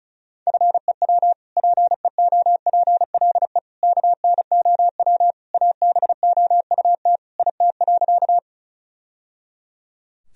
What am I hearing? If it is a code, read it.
Morse code, heard as a keyed tone, about 36 wpm, 700 Hz. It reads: FEW PEOPLE KNOW ABOUT IT.
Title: Sentences from Top 1000 Words - Review Part 1 - 35wpm